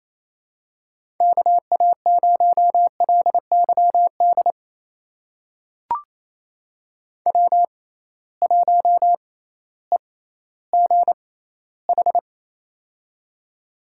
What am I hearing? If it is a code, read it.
Morse code, heard as a keyed tone, about 27 wpm, spaced as for 15 wpm, 700 Hz. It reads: KA0LYD W1EGH
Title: USA Call Signs - 6x Character Spacing - 28wpm